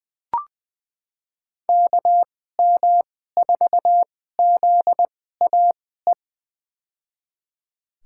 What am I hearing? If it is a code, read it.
Morse code, heard as a keyed tone, about 20 wpm, spaced as for 14 wpm, 700 Hz. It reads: KM4ZAE
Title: USA Call Signs - 2x Character Spacing - 20wpm